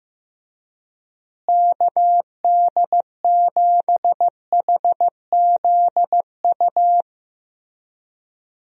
Morse code, heard as a keyed tone, about 15 wpm, 700 Hz. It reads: KD7HZU
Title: USA Call Signs - 6 Characters - Repeated 3x - 15wpm